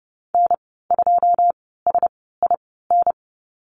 Morse code, heard as a keyed tone, about 30 wpm, spaced as for 17 wpm, 700 Hz. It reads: N2SIN
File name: course / 30:10wpm / 088 Lesson 15g - Number 5 - Call Signs 30wpm